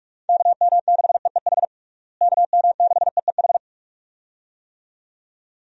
Morse code, heard as a keyed tone, about 41 wpm, 700 Hz. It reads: KM6EEH KM6EEH
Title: USA Call Signs - 6 Characters - Repeated 2x - 45wpm